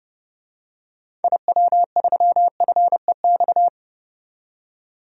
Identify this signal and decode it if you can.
Morse code, heard as a keyed tone, about 30 wpm, 700 Hz. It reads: IW3FEX
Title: DXCC-100 Call Signs - 6 Characters - 30wpm